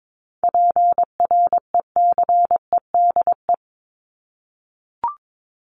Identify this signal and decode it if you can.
Morse code, heard as a keyed tone, about 22 wpm, 700 Hz. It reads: PRECEDE
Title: ING Suffix - Top 100 Words with Base Words 22wpm